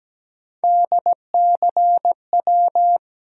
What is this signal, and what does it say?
Morse code, heard as a keyed tone, about 17 wpm, 700 Hz. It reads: DCW
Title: USA Call Sign Postfixes - 3 Letters 17wpm